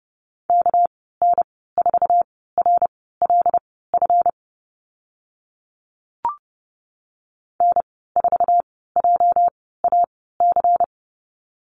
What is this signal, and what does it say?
Morse code, heard as a keyed tone, about 30 wpm, spaced as for 17 wpm, 700 Hz. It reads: KN4RLF N4JAC
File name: Lesson 37g - Letter x - Call Signs 30wpm